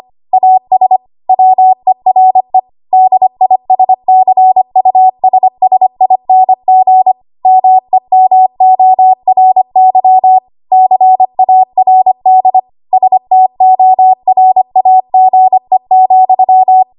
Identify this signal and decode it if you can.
Morse code, heard as a keyed tone, about 25 wpm, 750 Hz. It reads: AS WERE DISCUSSING MEMORY CARD STORAGE,
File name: Archive / 25 WPM / 25WPM